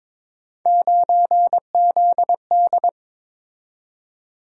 Morse code, heard as a keyed tone, about 26 wpm, 700 Hz. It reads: 9ZD
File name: Speed-Racing - Three Letter-Number Combinations - 22wpm